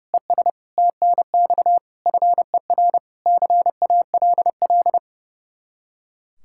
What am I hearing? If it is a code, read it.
Morse code, heard as a keyed tone, about 32 wpm, 700 Hz. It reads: ES TNX FER CALL
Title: Practice QSO - Exchange 1 - Beginning - 30wpm